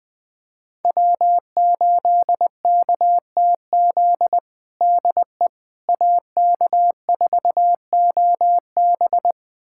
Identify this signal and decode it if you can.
Morse code, heard as a keyed tone, about 21 wpm, 700 Hz. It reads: W8KTZ DE AK4OB
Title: USA Callsign DE Callsign - 5 Characters - 20wpm